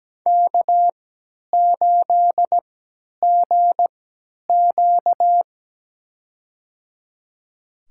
Morse code, heard as a keyed tone, about 17 wpm, spaced as for 10 wpm, 700 Hz. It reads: K8GQ
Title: USA Call Signs - 4 Characters - 3x Character Spacing - 17wpm